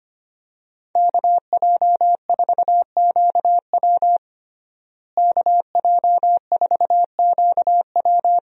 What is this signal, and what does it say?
Morse code, heard as a keyed tone, about 25 wpm, 700 Hz. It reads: KJ4QW KJ4QW
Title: USA Call Signs - 5 Characters - Repeated 3x - 25wpm